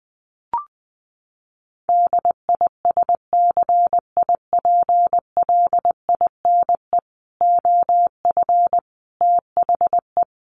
Morse code, heard as a keyed tone, about 20 wpm, 700 Hz. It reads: DISCIPLINE OF THE
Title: Sentences from Top 900 Words 20wpm